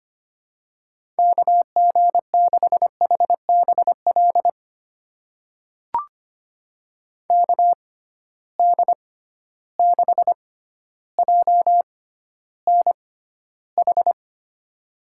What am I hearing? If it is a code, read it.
Morse code, heard as a keyed tone, about 25 wpm, spaced as for 13 wpm, 700 Hz. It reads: KG6HBL KD6JNH
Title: USA Call Signs - 6x Character Spacing - 25wpm